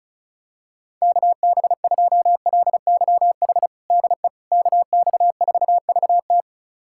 Morse code, heard as a keyed tone, about 36 wpm, 700 Hz. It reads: KB2LYH DE KX4VT
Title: USA Callsign DE Callsign 35wpm